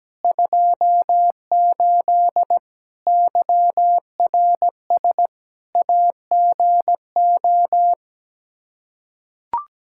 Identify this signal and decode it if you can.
Morse code, heard as a keyed tone, about 17 wpm, 700 Hz. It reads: 28 YRS AGO
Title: Practice QSO - Exchange 3 - More Info - 17wpm